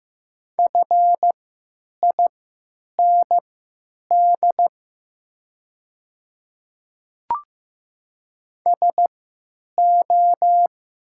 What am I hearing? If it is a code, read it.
Morse code, heard as a keyed tone, about 15 wpm, spaced as for 8 wpm, 700 Hz. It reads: FIND SO
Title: 3x Character Spacing - Top 200 Words - 15wpm